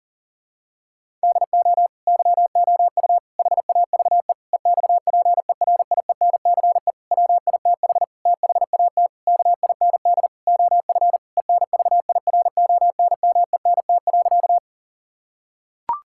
Morse code, heard as a keyed tone, about 40 wpm, 700 Hz. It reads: DO YOU HAVE EXPERIENCE WITH THAT KIND OF ENVIRONMENT.
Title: Sentences from Top 1000 Words - Review Part 5 - 40wpm